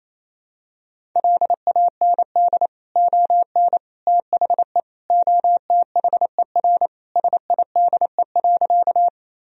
Morse code, heard as a keyed tone, about 27 wpm, 700 Hz. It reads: LAND ON THE OTHER SIDE.
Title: Sentences from Top 1000 Words - Review Part 1 - 28wpm